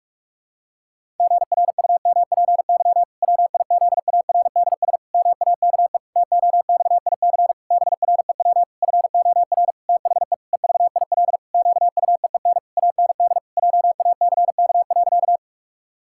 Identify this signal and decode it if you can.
Morse code, heard as a keyed tone, about 46 wpm, 700 Hz. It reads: GRUMPY WIZARDS MAKE TOXIC BREW FOR THE EVIL QUEEN AND JACK.
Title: Pangram Challenge v2 - 45wpm